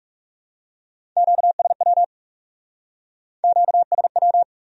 Morse code, heard as a keyed tone, about 37 wpm, 700 Hz. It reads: QSW QSW
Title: Speed-Racing Q-Codes - 30wpm